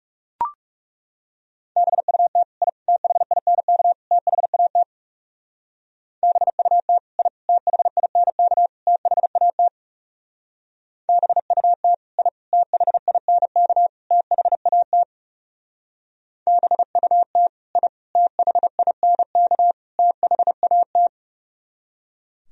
Morse code, heard as a keyed tone, about 37 wpm, 700 Hz. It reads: BUT I THINK THAT BUT I THINK THAT BUT I THINK THAT BUT I THINK THAT
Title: Speed-Racing - Sets of 4 Words 30wpm